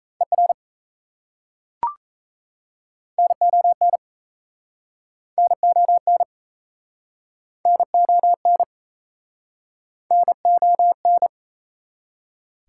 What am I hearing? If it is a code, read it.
Morse code, heard as a keyed tone, about 36 wpm, 700 Hz. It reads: ER NON NON NON NON